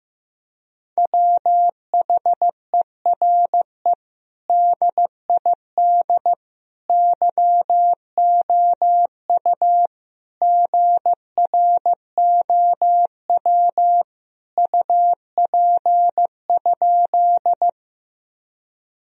Morse code, heard as a keyed tone, about 15 wpm, 700 Hz. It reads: WHERE DID YOU GROW UP?
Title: Sentences from Top 1000 Words - Review Part 1 - 15wpm